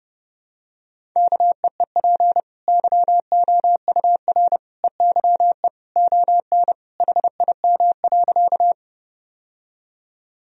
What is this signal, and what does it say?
Morse code, heard as a keyed tone, about 31 wpm, 700 Hz. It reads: KEEP YOUR EYE ON HIM.
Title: Sentences from Top 1000 Words - Review Part 1 - 30wpm